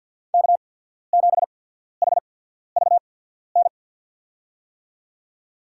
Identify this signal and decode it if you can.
Morse code, heard as a keyed tone, about 49 wpm, spaced as for 14 wpm, 700 Hz. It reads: K7HVN
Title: USA Call Signs - 5 Characters - 8x Character Spacing - 50wpm